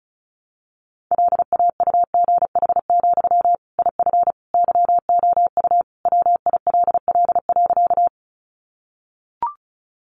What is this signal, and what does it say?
Morse code, heard as a keyed tone, about 34 wpm, 700 Hz. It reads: LAUGH, IF YOU WILL.
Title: Sentences from Top 500 Words 35wpm